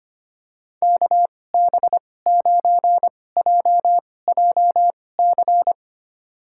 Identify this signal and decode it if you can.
Morse code, heard as a keyed tone, about 25 wpm, spaced as for 18 wpm, 700 Hz. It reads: KB9JJC
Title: USA Call Signs - 2x Character Spacing - 25wpm